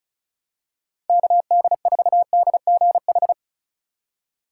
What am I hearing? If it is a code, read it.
Morse code, heard as a keyed tone, about 34 wpm, 700 Hz. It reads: KD4DGH
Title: USA Call Signs - 3x Character Spacing - 35wpm